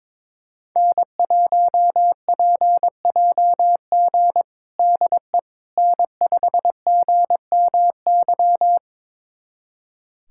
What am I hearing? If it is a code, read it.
Morse code, heard as a keyed tone, about 22 wpm, 700 Hz. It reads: N1PJG DE N5GMY
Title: USA Callsign DE Callsign - 5 Characters - 22wpm